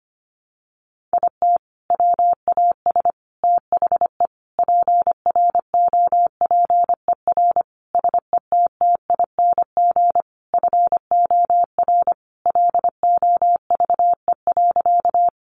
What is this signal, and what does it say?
Morse code, heard as a keyed tone, about 25 wpm, 700 Hz. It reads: IT WAS THE PROPER SETTING FOR LOVE.
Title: Sentences from Top 1300 Words 25wpm